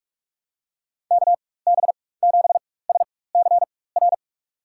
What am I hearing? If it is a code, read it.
Morse code, heard as a keyed tone, about 46 wpm, spaced as for 21 wpm, 700 Hz. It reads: KB7SCR
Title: USA Call Signs - 6 Characters - 4x Character Spacing - 45wpm